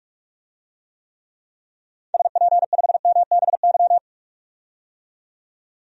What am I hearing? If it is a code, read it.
Morse code, heard as a keyed tone, about 42 wpm, 700 Hz. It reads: SP5MBY
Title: DXCC-100 Call Signs - 6 Characters - 45wpm